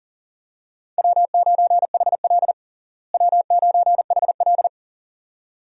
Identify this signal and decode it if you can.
Morse code, heard as a keyed tone, about 37 wpm, 700 Hz. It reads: W9HL W9HL
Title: USA Call Signs - 4 Characters - Repeated 2x - 40wpm